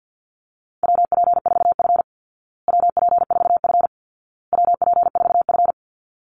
Morse code, heard as a keyed tone, about 50 wpm, 700 Hz. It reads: WP4F WP4F WP4F
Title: Call Signs Repeated 2x - 50wpm